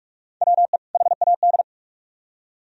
Morse code, heard as a keyed tone, about 44 wpm, 700 Hz. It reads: WE HAD